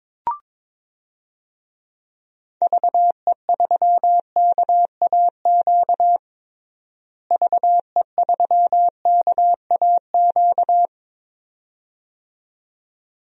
Morse code, heard as a keyed tone, about 21 wpm, 700 Hz. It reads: VE3KAQ VE3KAQ
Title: DXCC-100 Call Signs - 6 Characters - Repeated 2x - 22wpm